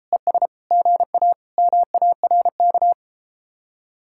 Morse code, heard as a keyed tone, about 34 wpm, 700 Hz. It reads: ES GA MARK